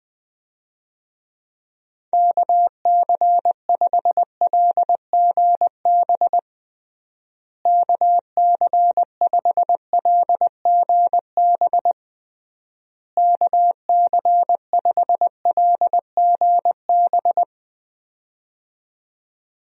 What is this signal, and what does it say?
Morse code, heard as a keyed tone, about 21 wpm, 700 Hz. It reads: KC5LGB KC5LGB KC5LGB
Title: USA Call Signs - 6 Characters - Repeated 3x - 20wpm